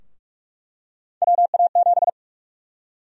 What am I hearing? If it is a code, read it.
Morse code, heard as a keyed tone, about 47 wpm, 700 Hz. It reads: WA7
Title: USA Call Sign Prefixes - 45wpm